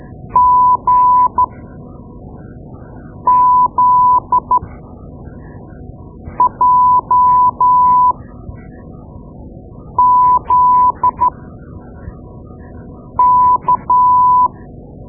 Morse code, heard as a keyed tone, about 12 wpm, spaced as for 4 wpm, 1000 Hz. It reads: GZJZK
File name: Morse Lessons / Morse Lessons / Tape-2B